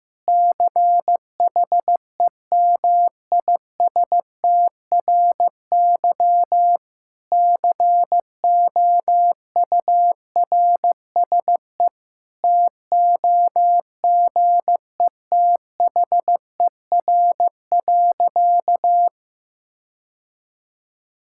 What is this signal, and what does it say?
Morse code, heard as a keyed tone, about 16 wpm, 700 Hz. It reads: CHEMISTRY COURSE TOGETHER.
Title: Sentences from Top 1000 Words - Review Part 6 - 15wpm